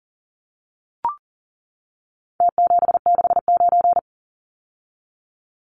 Morse code, heard as a keyed tone, about 40 wpm, 700 Hz. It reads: T769
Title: Numbers - Contest Style - 4-Digit 40wpm